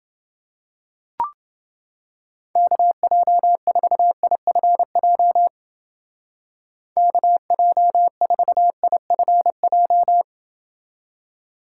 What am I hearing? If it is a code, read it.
Morse code, heard as a keyed tone, about 30 wpm, 700 Hz. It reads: KJ4IFJ KJ4IFJ